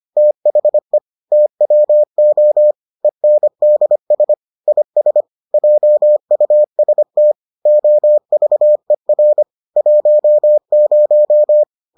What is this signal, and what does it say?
Morse code, heard as a keyed tone, about 25 wpm, 600 Hz. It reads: THE TWO ENDS IS JUST OVER 10